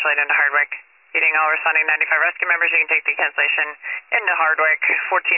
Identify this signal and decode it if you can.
Emergency Alert System (SAME) tones, heard as harsh buzzing bursts, with speech. It is heard from 0.3 to 4.2 s.